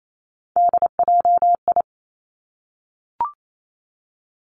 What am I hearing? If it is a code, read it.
Morse code, heard as a keyed tone, about 28 wpm, 700 Hz. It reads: DJI
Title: Three Challenging Letter-Number Combinations 28wpm